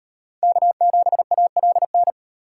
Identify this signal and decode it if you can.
Morse code, heard as a keyed tone, about 38 wpm, 700 Hz. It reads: K7ALN